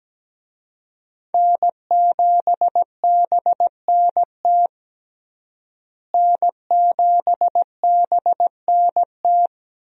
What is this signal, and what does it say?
Morse code, heard as a keyed tone, about 17 wpm, 700 Hz. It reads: N7BNT N7BNT
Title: USA Call Signs - 5 Characters - Repeated 2x - 17wpm